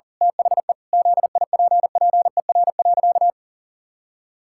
Morse code, heard as a keyed tone, about 40 wpm, 700 Hz. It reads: THE ZIPPER.
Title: Pangram Challenge v2 - 40wpm